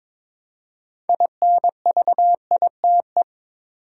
Morse code, heard as a keyed tone, about 21 wpm, 700 Hz. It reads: INVITE
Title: Six Letter Words - 22wpm